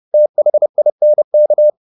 Morse code, heard as a keyed tone, about 30 wpm, 600 Hz. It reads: THINK
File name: the old man and the sea / the old man and the sea 1 30wpm